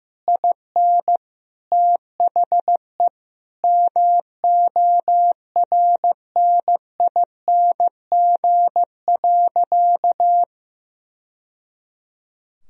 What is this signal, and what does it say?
Morse code, heard as a keyed tone, about 15 wpm, 700 Hz. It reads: IN THE MORNING.
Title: Sentences from Top 1000 Words - Review Part 2 - 15wpm